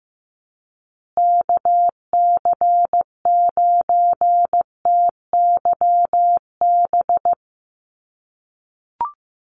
Morse code, heard as a keyed tone, about 15 wpm, 700 Hz. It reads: KC9TYB
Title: USA Call Signs Repeated 3x 15wpm